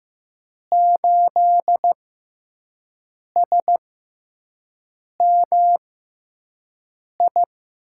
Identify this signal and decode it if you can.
Morse code, heard as a keyed tone, about 15 wpm, spaced as for 5 wpm, 700 Hz. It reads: 8SMI
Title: USA Call Signs - 6 Characters - 6x Character Spacing - 15wpm